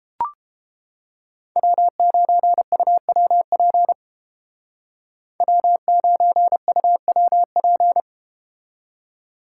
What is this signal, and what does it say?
Morse code, heard as a keyed tone, about 32 wpm, 700 Hz. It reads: W9UWP W9UWP